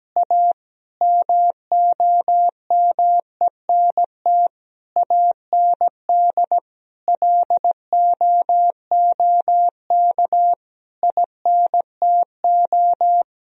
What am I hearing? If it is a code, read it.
Morse code, heard as a keyed tone, about 17 wpm, 700 Hz. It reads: A MOMENT AND LOOK INTO
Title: Sentences from Top 1000 Words - Review Part 1 - 17wpm